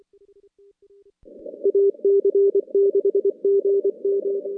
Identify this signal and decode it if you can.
Morse code, heard as a keyed tone, about 23 wpm, 400 Hz. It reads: AC6GM
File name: contests / CW-SS-k5tr- -2300